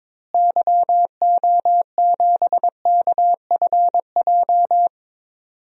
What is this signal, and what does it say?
Morse code, heard as a keyed tone, about 21 wpm, 700 Hz. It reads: YO7KFJ
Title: DXCC-100 Call Signs - 6 Characters - Repeated 2x - 22wpm